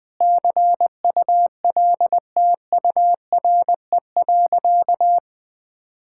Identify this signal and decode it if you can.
Morse code, heard as a keyed tone, about 20 wpm, 700 Hz. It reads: CULTURE.
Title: Sentences from Top 1000 Words - Review Part 6 - 20wpm